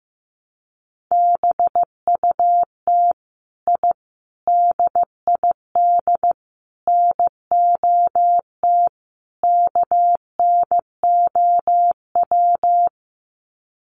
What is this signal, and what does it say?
Morse code, heard as a keyed tone, about 15 wpm, 700 Hz. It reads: BUT I DID NOT KNOW
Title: Sets of 5 Words Encore 15wpm